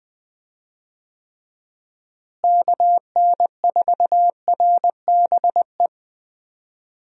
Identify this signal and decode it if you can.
Morse code, heard as a keyed tone, about 21 wpm, 700 Hz. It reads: KN4RBE
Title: USA Call Signs - 6 Characters - Repeated 3x - 20wpm